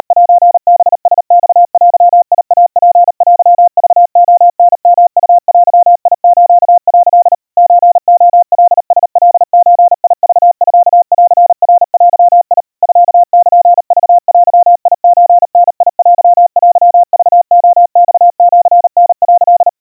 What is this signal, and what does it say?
Morse code (Japanese Wabun code), heard as a keyed tone, about 38 wpm, 700 Hz. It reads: セハラマデイツテクレタヨウデス」 ソレカラカゾクミンナデ ミルクデソダテテクレマシタ」
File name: kawa / Morse sample / kamyu 140 cpm 700 Hz Crispy